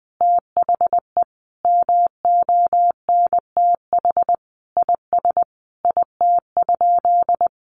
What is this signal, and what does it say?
Morse code, heard as a keyed tone, about 20 wpm, 700 Hz. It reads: THE MONTH IS IT?
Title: Sentences from Top 800 Words 20wpm